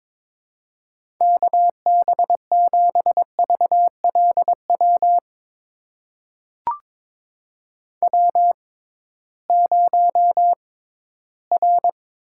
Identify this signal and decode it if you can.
Morse code, heard as a keyed tone, about 22 wpm, spaced as for 14 wpm, 700 Hz. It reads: KB7VLW W0R
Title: USA Call Signs - 6x Character Spacing - 22wpm